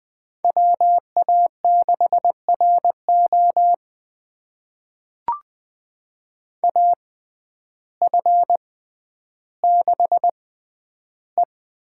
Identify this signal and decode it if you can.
Morse code, heard as a keyed tone, about 20 wpm, spaced as for 11 wpm, 700 Hz. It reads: WA6RO AF6E
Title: USA Call Signs - 5 Characters - 6x Character Spacing - 20wpm